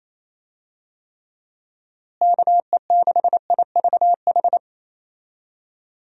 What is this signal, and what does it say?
Morse code, heard as a keyed tone, about 28 wpm, 700 Hz. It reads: KE6IVH
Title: USA Call Signs - 6 Characters - Repeated 2x - 28wpm